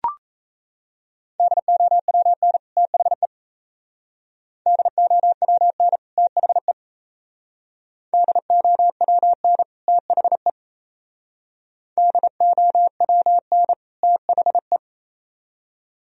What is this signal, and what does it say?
Morse code, heard as a keyed tone, about 35 wpm, 700 Hz. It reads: DOWN THE DOWN THE DOWN THE DOWN THE
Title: Speed-Racing - Sets of 2 Words 28wpm